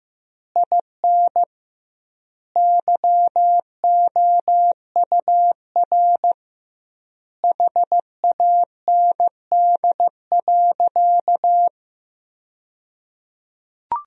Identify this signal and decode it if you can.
Morse code, heard as a keyed tone, about 15 wpm, 700 Hz. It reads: IN YOUR HAND.
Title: 2x Word-Spacing - Sentences from Top 400 Words - 15wpm